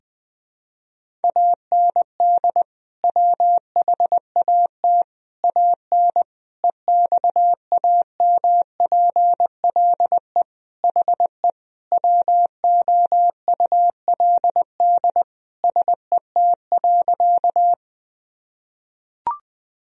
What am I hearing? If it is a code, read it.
Morse code, heard as a keyed tone, about 20 wpm, 700 Hz. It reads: AND WHAT AN EXAMPLE HE WOULD SET.
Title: Sentences from Top 1000 Words - Review Part 1 - 20wpm